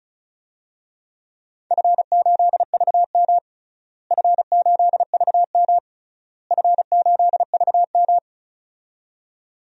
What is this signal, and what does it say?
Morse code, heard as a keyed tone, about 34 wpm, 700 Hz. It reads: F8VM F8VM F8VM
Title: DXCC-100 Call Signs - 4 Characters - Repeated 3x - 35wpm